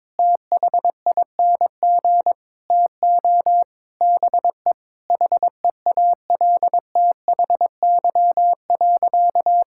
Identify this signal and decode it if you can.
Morse code, heard as a keyed tone, about 21 wpm, 700 Hz. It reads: THING TO BE HEALTHY.
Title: Sentences from Top 1000 Words - Review Part 5 - 22wpm